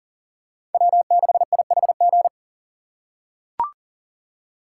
Morse code, heard as a keyed tone, about 40 wpm, 700 Hz. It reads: W6IHG
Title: K1USN SST QSO Practice - 40wpm